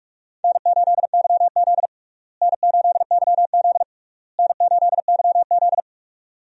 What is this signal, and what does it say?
Morse code, heard as a keyed tone, about 46 wpm, 700 Hz. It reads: N8YZ N8YZ N8YZ
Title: USA Call Signs - 4 Characters - Repeated 3x - 45wpm